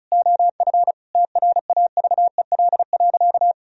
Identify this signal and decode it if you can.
Morse code, heard as a keyed tone, about 36 wpm, 700 Hz. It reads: OF TRAVEL.
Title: Sentences from Top 1000 Words - Review Part 4 - 35wpm